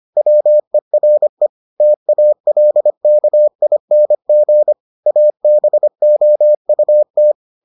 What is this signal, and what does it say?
Morse code, heard as a keyed tone, about 25 wpm, 600 Hz. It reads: WERE TALKING ABOUT